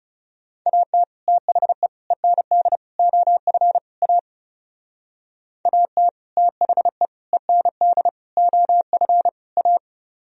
Morse code, heard as a keyed tone, about 36 wpm, 700 Hz. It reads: AT THE END OF A AT THE END OF A
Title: Speed-Racing - Sets of 5 Words 30wpm